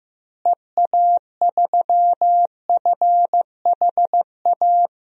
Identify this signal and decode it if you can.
Morse code, heard as a keyed tone, about 15 wpm, 700 Hz. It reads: EA3FHA
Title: DXCC-100 Call Signs - 6 Characters - Repeated 3x - 15wpm